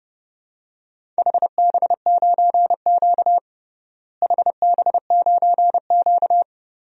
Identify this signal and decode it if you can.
Morse code, heard as a keyed tone, about 29 wpm, 700 Hz. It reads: HB9Q HB9Q
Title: DXCC-100 Call Signs - 4 Characters - Repeated 2x - 30wpm